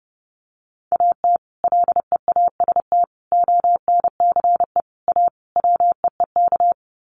Morse code, heard as a keyed tone, about 30 wpm, 700 Hz. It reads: AT LEAST ONCE A WEEK
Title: Sets of 5 Words Encore2 30wpm